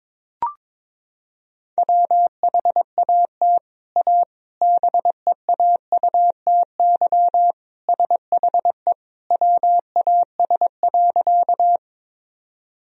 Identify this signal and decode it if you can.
Morse code, heard as a keyed tone, about 22 wpm, 700 Hz. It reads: WHAT A BEAUTY SHE WAS.
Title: Sentences from Top 1000 Words - Review Part 2 - 22wpm